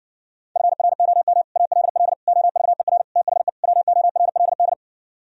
Morse code, heard as a keyed tone, about 59 wpm, 700 Hz. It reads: FROM ALL OVER THE WORLD